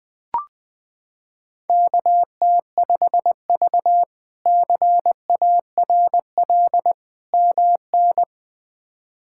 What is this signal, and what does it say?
Morse code, heard as a keyed tone, about 20 wpm, 700 Hz. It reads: KT5V CARL MN
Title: K1USN SST QSO Practice - 20wpm